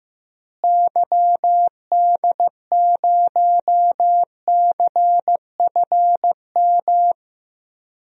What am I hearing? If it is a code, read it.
Morse code, heard as a keyed tone, about 15 wpm, 700 Hz. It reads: YD0CFM
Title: DXCC-100 Call Signs - 6 Characters - Repeated 3x - 15wpm